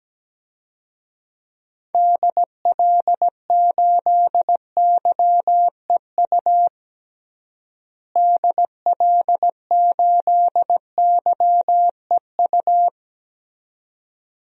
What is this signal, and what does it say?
Morse code, heard as a keyed tone, about 17 wpm, 700 Hz. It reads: DL8YEU DL8YEU
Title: DXCC-100 Call Signs - 6 Characters - Repeated 2x - 17wpm